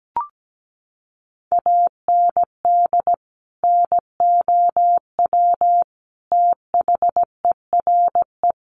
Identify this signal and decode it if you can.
Morse code, heard as a keyed tone, about 17 wpm, 700 Hz. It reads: AND NOW THERE
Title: Sentences from Top 1400 Words 17wpm